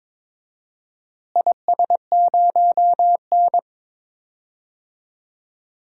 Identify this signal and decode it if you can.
Morse code, heard as a keyed tone, about 22 wpm, 700 Hz. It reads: IS0N
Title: DXCC-100 Call Signs - 4 Characters - Repeated 2x - 22wpm